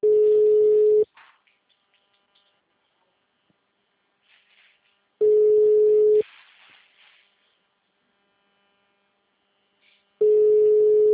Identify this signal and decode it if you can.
Ringback tone (Europe, 425 Hz).